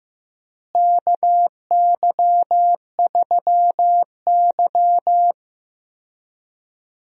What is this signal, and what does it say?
Morse code, heard as a keyed tone, about 15 wpm, 700 Hz. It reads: KY3Y
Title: USA Call Signs - 4 Characters - Repeated 2x - 15wpm